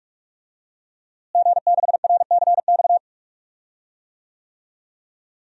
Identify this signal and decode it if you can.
Morse code, heard as a keyed tone, about 46 wpm, 700 Hz. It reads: G6RCX
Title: DXCC-100 Call Signs - 5 Characters - Repeated 3x - 45wpm